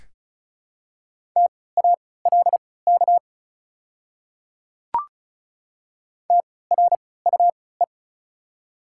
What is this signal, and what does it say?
Morse code, heard as a keyed tone, about 36 wpm, spaced as for 20 wpm, 690 Hz. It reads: TALK TRUE
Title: 3x Character Spacing - Top 500 Words - 35wpm